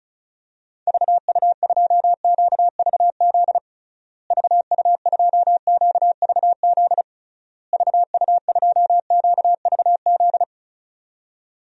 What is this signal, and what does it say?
Morse code, heard as a keyed tone, about 34 wpm, 700 Hz. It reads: VU2QVZ VU2QVZ VU2QVZ
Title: DXCC-100 Call Signs - 6 Characters - Repeated 3x - 35wpm